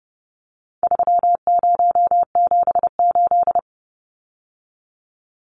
Morse code, heard as a keyed tone, about 30 wpm, 700 Hz. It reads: 3078
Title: Numbers - 4-Digit 30wpm